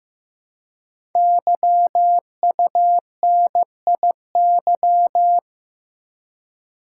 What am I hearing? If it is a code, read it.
Morse code, heard as a keyed tone, about 15 wpm, 700 Hz. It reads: YUNIY
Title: Code Groups - 5 Letters - 15wpm